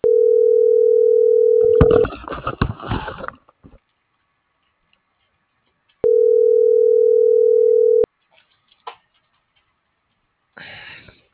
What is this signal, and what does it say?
Ringback tone (North America), 440 + 480 Hz.